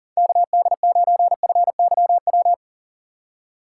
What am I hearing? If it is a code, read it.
Morse code, heard as a keyed tone, about 40 wpm, 700 Hz. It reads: KD9FYW
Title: USA Call Signs - 6 Characters - 2x Character Spacing - 40wpm